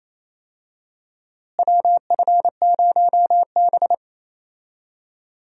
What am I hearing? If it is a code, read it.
Morse code, heard as a keyed tone, about 28 wpm, 700 Hz. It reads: WF0B
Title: USA Call Signs - 4 Characters - Repeated 3x - 28wpm